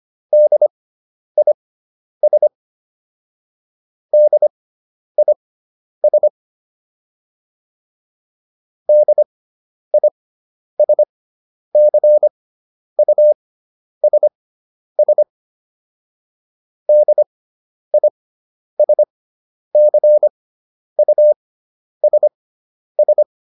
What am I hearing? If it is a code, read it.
Morse code, heard as a keyed tone, about 25 wpm, spaced as for 10 wpm, 600 Hz. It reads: DIS DIS DISCUSS DISCUSS